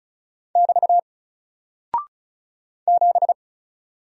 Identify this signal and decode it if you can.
Morse code, heard as a keyed tone, about 36 wpm, 700 Hz. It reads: <BT> 7
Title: Single Character (Letter:Number:Punctuation:Prosign) Alt 35wpm